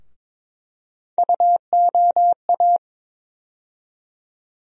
Morse code, heard as a keyed tone, about 22 wpm, 700 Hz. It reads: UOA
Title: USA Call Sign Postfixes - 3 Letters 22wpm